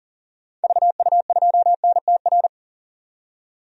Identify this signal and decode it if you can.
Morse code, heard as a keyed tone, about 40 wpm, 700 Hz. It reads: VU2NTR
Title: DXCC-100 Call Signs - 6 Characters - 40wpm